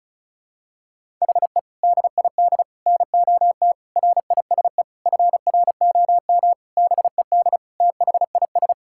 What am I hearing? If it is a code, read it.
Morse code, heard as a keyed tone, about 34 wpm, 700 Hz. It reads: HE DID NOT RISE FROM BED THIS
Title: Sentences from Top 1000 Words - Review Part 6 - 35wpm